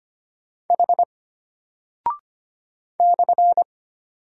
Morse code, heard as a keyed tone, about 25 wpm, 700 Hz. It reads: H /
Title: Single Character (Letter:Number:Punctuation:Prosign) Alt 25wpm